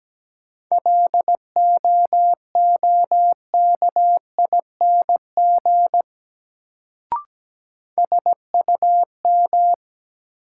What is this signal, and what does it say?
Morse code, heard as a keyed tone, about 17 wpm, 700 Hz. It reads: LOOKING SUM
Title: Sentences from Top 1000 Words - Review Part 6 - New Word Drill - 17wpm